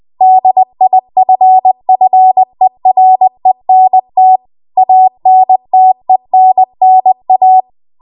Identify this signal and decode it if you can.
Morse code, heard as a keyed tone, about 20 wpm, 750 Hz. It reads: DIFFERENT ANTENNA
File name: Archive / 20 WPM / 20WPM